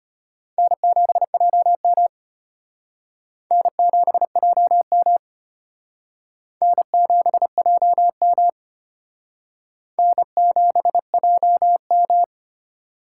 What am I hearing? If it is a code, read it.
Morse code, heard as a keyed tone, about 32 wpm, 700 Hz. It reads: N7JM N7JM N7JM N7JM